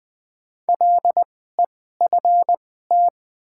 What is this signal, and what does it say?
Morse code, heard as a keyed tone, about 21 wpm, spaced as for 15 wpm, 700 Hz. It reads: LEFT
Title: 2x Character Spacing - Top 300 Words - 20wpm